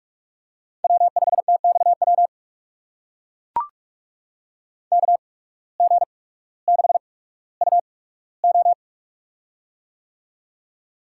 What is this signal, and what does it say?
Morse code, heard as a keyed tone, about 41 wpm, spaced as for 19 wpm, 700 Hz. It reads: W5TXW KG6UO
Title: USA Call Signs - 5 Characters - 8x Character Spacing - 45wpm